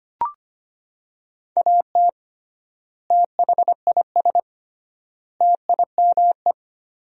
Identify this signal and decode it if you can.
Morse code, heard as a keyed tone, about 25 wpm, 700 Hz. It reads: AT THIS TIME
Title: 3x Word-Spacing - Sentences from Top 300 Words - 25wpm